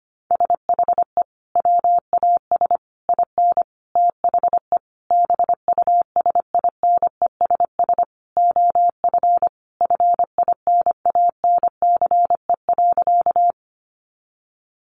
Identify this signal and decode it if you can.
Morse code, heard as a keyed tone, about 25 wpm, 700 Hz. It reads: SHE WAS IN THE BUSINESS OF FINANCE.
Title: Sentences from Top 1500 Words 25wpm